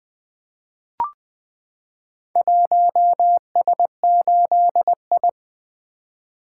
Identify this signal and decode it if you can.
Morse code, heard as a keyed tone, about 20 wpm, 700 Hz. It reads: 1S8I
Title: Four Letter-Number Combinations - 20wpm